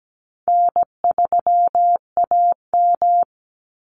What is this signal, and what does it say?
Morse code, heard as a keyed tone, about 17 wpm, 700 Hz. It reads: N3AM
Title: Call Signs - 17wpm